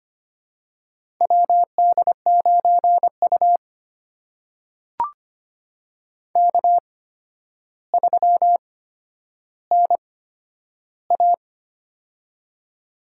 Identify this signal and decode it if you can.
Morse code, heard as a keyed tone, about 25 wpm, spaced as for 11 wpm, 700 Hz. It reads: WD9U K3NA
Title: USA Call Signs - 4 Characters - 8x Character Spacing - 25wpm